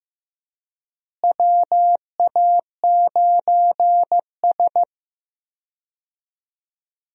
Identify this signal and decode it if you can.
Morse code, heard as a keyed tone, about 15 wpm, 700 Hz. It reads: WA9S
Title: USA Call Signs - 4 Characters - Repeated 3x - 15wpm